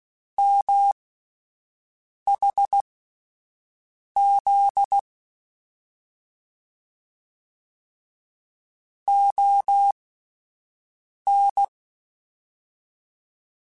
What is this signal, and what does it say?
Morse code, heard as a keyed tone, about 16 wpm, spaced as for 5 wpm, 800 Hz. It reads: MHZ ON
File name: QSOs Spaced